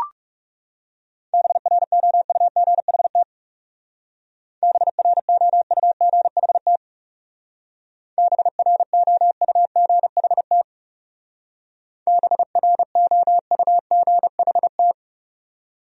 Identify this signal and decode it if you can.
Morse code, heard as a keyed tone, about 37 wpm, 700 Hz. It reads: BROUGHT BROUGHT BROUGHT BROUGHT
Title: Speed-Racing Top 500 Words 30wpm